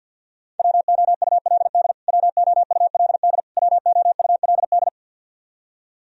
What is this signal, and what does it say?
Morse code, heard as a keyed tone, about 52 wpm, 700 Hz. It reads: WOULD WOULD WOULD
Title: Top 100 Words Repeated 50wpm